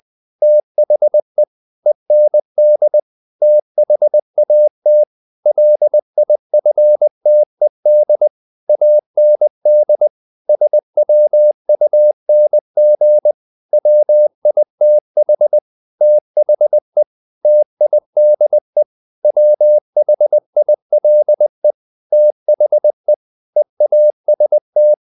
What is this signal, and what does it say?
Morse code, heard as a keyed tone, about 20 wpm, 600 Hz. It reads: THE END THAT LIFTED AND SWUNG WITH THE TIDE WHILE THE EAST
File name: the old man and the sea part 2 20wpm